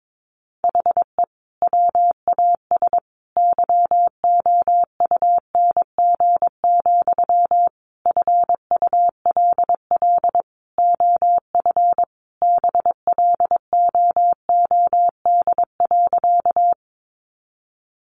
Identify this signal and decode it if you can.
Morse code, heard as a keyed tone, about 22 wpm, 700 Hz. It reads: HE WAS YOUNG, FULL OF BLOOD.
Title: Sentences from Top 1100 Words 22wpm